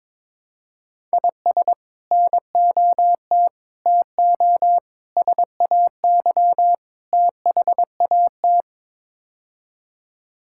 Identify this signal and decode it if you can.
Morse code, heard as a keyed tone, about 22 wpm, 700 Hz. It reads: IS NOT TO SAY THAT
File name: practice / 22wpm / Build A Phrase - 5-Word Phrases - 22wpm